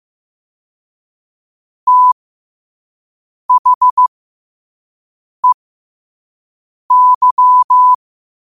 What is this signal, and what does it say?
Morse code, heard as a keyed tone, about 15 wpm, spaced as for 5 wpm, 1000 Hz. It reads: THEY
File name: qotdfiles / 5wpm